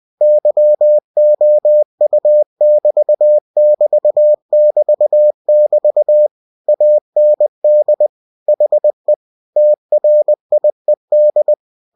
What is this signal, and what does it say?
Morse code, heard as a keyed tone, about 20 wpm, 600 Hz. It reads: YOU<BT><BT><BT><BT> AND HE TRIED
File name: cw-books / winnie-the-pooh / winnie-20wpm-page-0015